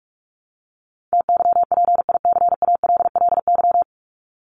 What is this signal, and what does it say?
Morse code, heard as a keyed tone, about 44 wpm, 700 Hz. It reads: TYPICALLY
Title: LY Suffix - Top 100 Words with Base Words 45wpm